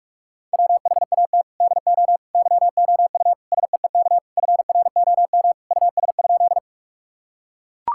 Morse code, heard as a keyed tone, about 42 wpm, 700 Hz. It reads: WHAT DO YOU SEEK FROM US?
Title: Sentences from Top 1600 Words 45wpm